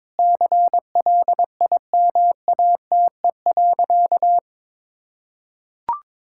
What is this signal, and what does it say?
Morse code, heard as a keyed tone, about 22 wpm, 700 Hz. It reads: CLIMATE.
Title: Sentences from Top 1600 Words 22wpm